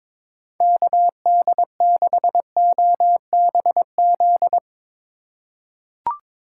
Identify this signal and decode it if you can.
Morse code, heard as a keyed tone, about 22 wpm, 700 Hz. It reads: KD6OBZ
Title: USA Call Signs - 6 Characters - 2x Character Spacing - 22wpm